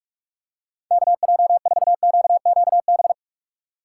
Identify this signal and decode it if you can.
Morse code, heard as a keyed tone, about 44 wpm, 700 Hz. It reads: KJ4QQB
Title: USA Call Signs - 6x Character Spacing - 45wpm